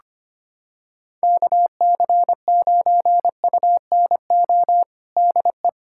Morse code, heard as a keyed tone, about 25 wpm, 700 Hz. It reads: KC9UNO DE
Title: USA Callsign DE Callsign 25wpm